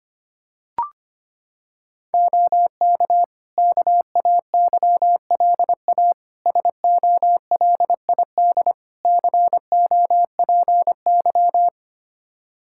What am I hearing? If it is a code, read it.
Morse code, heard as a keyed tone, about 25 wpm, 700 Hz. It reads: OK KAYLA SOLID COPY
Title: K1USN SST QSO Practice 25wpm